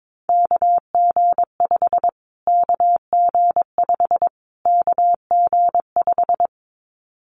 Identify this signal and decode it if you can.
Morse code, heard as a keyed tone, about 22 wpm, 700 Hz. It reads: KG5 KG5 KG5
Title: Callsign Prefixes for Top 25 DXCC - Repeated 3x - 22wpm